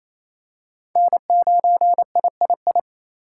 Morse code, heard as a keyed tone, about 28 wpm, 700 Hz. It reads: N9III
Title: USA Call Signs - 5 Characters - 2x Character Spacing - 28wpm